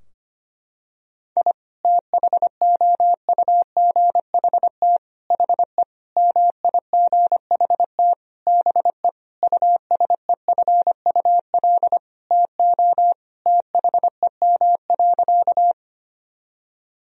Morse code, heard as a keyed tone, about 25 wpm, 700 Hz. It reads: I THOUGHT HE MIGHT BE USEFUL TO THEM.
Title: Sentences from Top 1000 Words - Review Part 4 - 25wpm